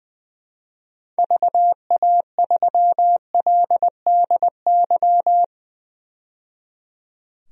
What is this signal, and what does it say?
Morse code, heard as a keyed tone, about 20 wpm, 700 Hz. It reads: VA3LDY
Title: DXCC-100 Call Signs - 6 Characters - 20wpm